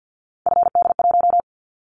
Morse code, heard as a keyed tone, about 50 wpm, 700 Hz. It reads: FD1
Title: Callsign Prefixes for Top 50 DXCC - 50wpm